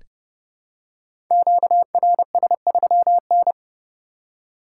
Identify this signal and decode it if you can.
Morse code, heard as a keyed tone, about 29 wpm, 700 Hz. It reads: QRS3N
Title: Code Groups - 5 Letter:Numbers - 30wpm